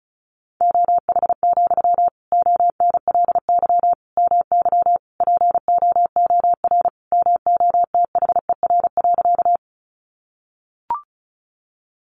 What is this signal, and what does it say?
Morse code, heard as a keyed tone, about 34 wpm, 700 Hz. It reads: OH, ONLY MY POOR MOTHER.
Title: Sentences from Top 1000 Words 35wpm